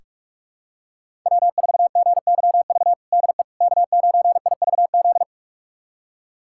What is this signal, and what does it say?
Morse code, heard as a keyed tone, about 41 wpm, 700 Hz. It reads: W4GYV DE K9IVZ
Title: USA Callsign DE Callsign - 5 Characters - 45wpm